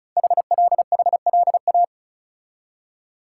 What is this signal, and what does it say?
Morse code, heard as a keyed tone, about 36 wpm, 700 Hz. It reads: HLHLA
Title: Code Groups - 5 Letters - 35wpm